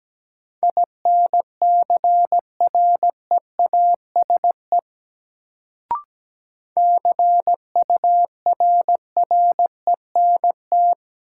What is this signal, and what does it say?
Morse code, heard as a keyed tone, about 17 wpm, 700 Hz. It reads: INCREASE CURRENT
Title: Sentences from Top 1000 Words - Review Part 4 - New Word Drill - 17wpm